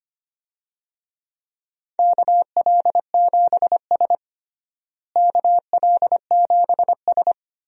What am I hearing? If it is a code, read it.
Morse code, heard as a keyed tone, about 25 wpm, 700 Hz. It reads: KL7S KL7S
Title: DXCC-100 Call Signs - 4 Characters - Repeated 2x - 25wpm